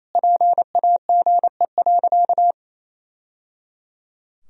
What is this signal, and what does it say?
Morse code, heard as a keyed tone, about 28 wpm, 700 Hz. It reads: PAGE.
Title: Sentences from Top 1000 Words - Review Part 1 - 28wpm